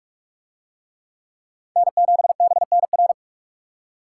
Morse code, heard as a keyed tone, about 41 wpm, 700 Hz. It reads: N7BNR